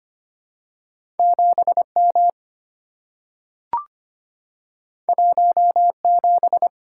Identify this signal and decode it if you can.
Morse code, heard as a keyed tone, about 25 wpm, 700 Hz. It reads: 7M 17
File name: Two Letter-Number Combinations - 25wpm